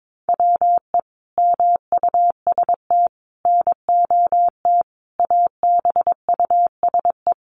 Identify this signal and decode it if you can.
Morse code, heard as a keyed tone, about 22 wpm, 700 Hz. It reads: WE MUST NOT ABUSE
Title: Sentences from Top 1300 Words 22wpm